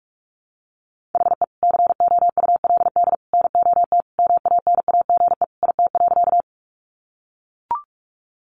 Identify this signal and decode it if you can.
Morse code, heard as a keyed tone, about 44 wpm, 700 Hz. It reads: HE COULD NOT MANAGE IT.
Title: Sentences from Top 1400 Words 45wpm